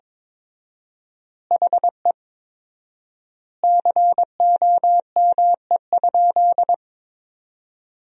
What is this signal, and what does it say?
Morse code, heard as a keyed tone, about 22 wpm, 700 Hz. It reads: HE COME?
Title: 4x Word-Spacing - Sentences from Top 200 Words - 22wpm